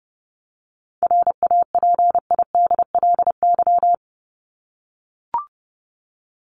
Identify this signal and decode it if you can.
Morse code, heard as a keyed tone, about 30 wpm, 700 Hz. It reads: RAPIDLY
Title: LY Suffix - Top 100 Words with Base Words 30wpm